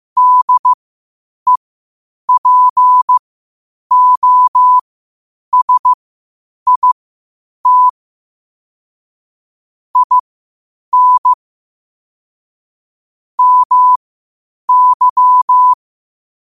Morse code, heard as a keyed tone, about 15 wpm, spaced as for 8 wpm, 1000 Hz. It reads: DEPOSIT IN MY
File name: qotdfiles / 8wpm